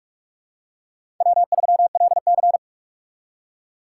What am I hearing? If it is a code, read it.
Morse code, heard as a keyed tone, about 46 wpm, 700 Hz. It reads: W3LC
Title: USA Call Signs - 4 Characters - 2x Character Spacing - 45wpm